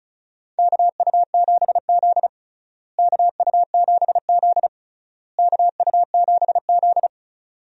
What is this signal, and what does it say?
Morse code, heard as a keyed tone, about 36 wpm, 700 Hz. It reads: KU7Z KU7Z KU7Z
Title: USA Call Signs - 4 Characters - Repeated 3x - 35wpm